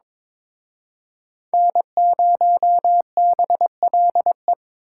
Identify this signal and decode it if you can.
Morse code, heard as a keyed tone, about 26 wpm, 700 Hz. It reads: N0BLE
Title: Speed Racing - USA Call Signs 22wpm